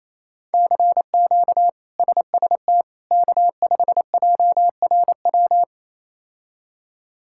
Morse code, heard as a keyed tone, about 28 wpm, 700 Hz. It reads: CQ SST K5JRW
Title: K1USN SST QSO Practice - 28wpm